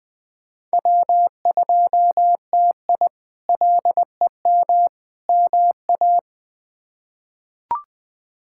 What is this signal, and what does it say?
Morse code, heard as a keyed tone, about 20 wpm, 700 Hz. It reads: W2TI LEM MA
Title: K1USN SST QSO Practice - 20wpm